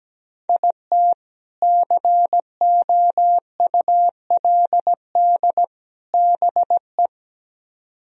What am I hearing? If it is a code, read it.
Morse code, heard as a keyed tone, about 17 wpm, 700 Hz. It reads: IT COULD BE
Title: Speed-Racing - Sets of 3 Words 17wpm